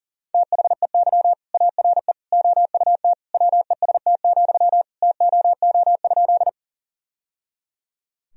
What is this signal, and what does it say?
Morse code, heard as a keyed tone, about 40 wpm, 700 Hz. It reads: THEY ARE OUT WEST, TOO?
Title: Sentences from Top 1000 Words - Review Part 2 - 40wpm